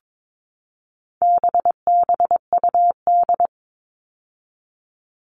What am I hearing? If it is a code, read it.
Morse code, heard as a keyed tone, about 22 wpm, 700 Hz. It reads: BBUD
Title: All Cut Numbers - 4-Digit 22wpm